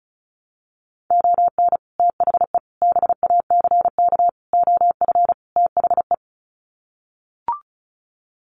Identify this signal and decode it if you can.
Morse code, heard as a keyed tone, about 34 wpm, 700 Hz. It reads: ON THE BACK OF THE
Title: Sets of 5 Words 35wpm